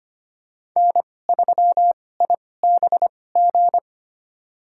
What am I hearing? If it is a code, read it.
Morse code, heard as a keyed tone, about 25 wpm, spaced as for 18 wpm, 700 Hz. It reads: N3IBG
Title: USA Call Signs - 5 Characters - 2x Character Spacing - 25wpm